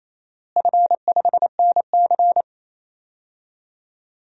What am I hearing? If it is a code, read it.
Morse code, heard as a keyed tone, about 28 wpm, 700 Hz. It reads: F5NC
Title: DXCC-100 Call Signs - 4 Characters - Repeated 3x - 28wpm